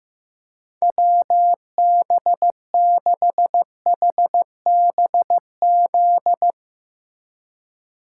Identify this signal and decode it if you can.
Morse code, heard as a keyed tone, about 15 wpm, 700 Hz. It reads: WB6HBZ
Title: USA Call Signs - 6 Characters - 2x Character Spacing - 15wpm